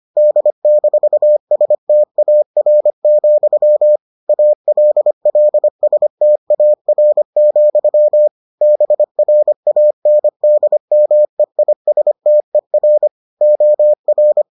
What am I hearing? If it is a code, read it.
Morse code, heard as a keyed tone, about 25 wpm, 600 Hz. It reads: D-STAR, ALLSTAR, BRANDMEISTER OR